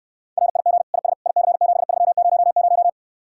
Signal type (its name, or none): Morse code